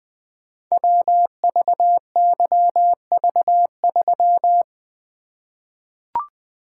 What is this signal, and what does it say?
Morse code, heard as a keyed tone, about 20 wpm, 700 Hz. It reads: WVYV3
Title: Code Groups - 5 Letter:Numbers - 20wpm